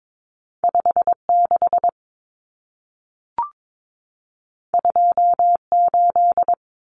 Morse code, heard as a keyed tone, about 22 wpm, 700 Hz. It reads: 56 28
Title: Numbers - 2-Digit 22wpm